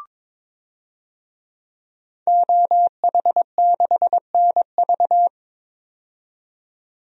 Morse code, heard as a keyed tone, about 22 wpm, 700 Hz. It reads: OH6NV
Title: DXCC-100 Call Signs - 5 Characters - 22wpm